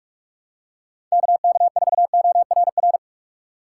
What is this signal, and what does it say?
Morse code, heard as a keyed tone, about 44 wpm, 700 Hz. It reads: KK4ORR
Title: USA Call Signs - 6 Characters - 4x Character Spacing - 45wpm